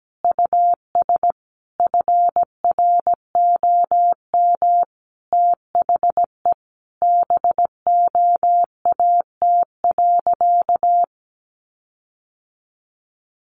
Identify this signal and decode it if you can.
Morse code, heard as a keyed tone, about 17 wpm, 700 Hz. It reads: US FROM THE BOAT.
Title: Sentences from Top 1400 Words 17wpm